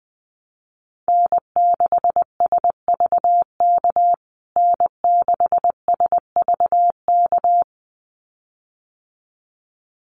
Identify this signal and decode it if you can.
Morse code, heard as a keyed tone, about 20 wpm, 700 Hz. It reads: N6SVK N6SVK
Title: USA Call Signs Repeated 2x 20wpm